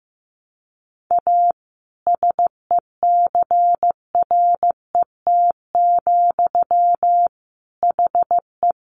Morse code, heard as a keyed tone, about 15 wpm, 700 Hz. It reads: A SECRET, HE
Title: Sentences from Top 1200 Words 15wpm